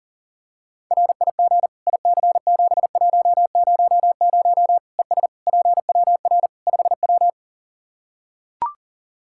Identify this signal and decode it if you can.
Morse code, heard as a keyed tone, about 37 wpm, 700 Hz. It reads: RIG IC7100 ES PWR 5W
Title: Practice QSO - Exchange 2 - Info - 40wpm